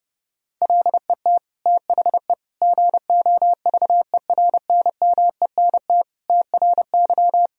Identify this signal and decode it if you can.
Morse code, heard as a keyed tone, about 31 wpm, 700 Hz. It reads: LET THE GOVERNMENT TRY
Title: Sentences from Top 1000 Words - Review Part 3 - 30wpm